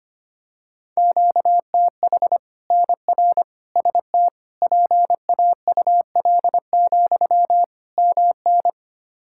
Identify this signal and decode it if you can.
Morse code, heard as a keyed tone, about 25 wpm, 700 Hz. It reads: QTH NR ST PAUL, MN
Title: K1USN SST QSO Practice 25wpm